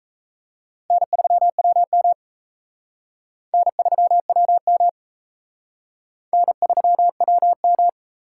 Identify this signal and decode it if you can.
Morse code, heard as a keyed tone, about 33 wpm, 700 Hz. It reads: N3WM N3WM N3WM
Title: Speed Racing - USA 1x2 Call Signs 28wpm